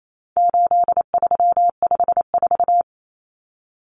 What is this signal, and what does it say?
Morse code, heard as a keyed tone, about 28 wpm, 700 Hz. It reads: 8354
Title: Numbers - 4-Digit 28wpm